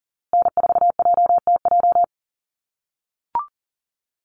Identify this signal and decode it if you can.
Morse code, heard as a keyed tone, about 40 wpm, 700 Hz. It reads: N4JTJ
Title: USA Call Signs Repeated 3x 40wpm